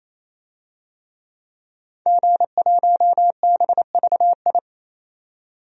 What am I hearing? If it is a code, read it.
Morse code, heard as a keyed tone, about 28 wpm, 700 Hz. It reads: G1BVI
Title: DXCC-100 Call Signs - 5 Characters - 28wpm